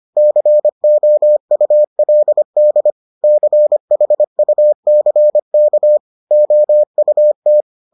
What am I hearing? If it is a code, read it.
Morse code, heard as a keyed tone, about 25 wpm, 600 Hz. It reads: COULD CHUCK OUT